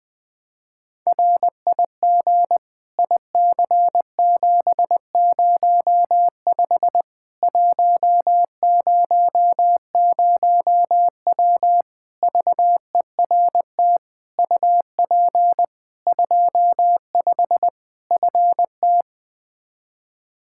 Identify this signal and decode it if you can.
Morse code, heard as a keyed tone, about 21 wpm, 700 Hz. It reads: RIG IC705 100W VERT UP 25 FT
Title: K1USN SST QSO Practice 20wpm